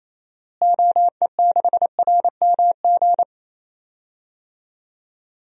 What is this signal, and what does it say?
Morse code, heard as a keyed tone, about 28 wpm, 700 Hz. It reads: OE6RMG
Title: DXCC-100 Call Signs - 6 Characters - Repeated 2x - 28wpm